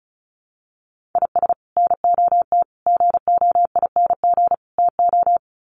Morse code, heard as a keyed tone, about 34 wpm, 700 Hz. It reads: IS NOT GOING TO
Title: Sets of 4 Words 35wpm